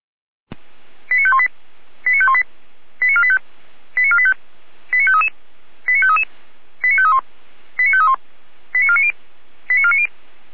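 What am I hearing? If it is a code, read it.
Five-tone selective call (ZVEI), heard as quick runs of five tones, tones of about 70 ms. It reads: 89518 89518 89485 89485 89533 89533 89531 89531 89490 89490
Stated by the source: ZVEI five-tone selective call received on an SDR.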